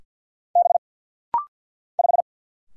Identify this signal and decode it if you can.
Morse code, heard as a keyed tone, about 50 wpm, 700 Hz. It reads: B 5
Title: Single Letter-Number - Rapid-Fire 50wpm